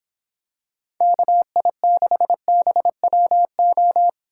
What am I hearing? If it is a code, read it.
Morse code, heard as a keyed tone, about 26 wpm, 700 Hz. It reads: KI6BWO